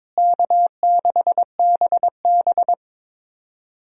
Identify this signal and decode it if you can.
Morse code, heard as a keyed tone, about 22 wpm, 700 Hz. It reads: K6BB
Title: USA Call Signs - 4 Characters - Repeated 3x - 22wpm